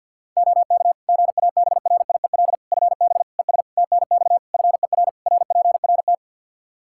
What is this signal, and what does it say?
Morse code, heard as a keyed tone, about 47 wpm, 700 Hz. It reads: OK GABRIEL FB ES TNX FER RPRT
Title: Practice QSO - Exchange 2 - Info - 50wpm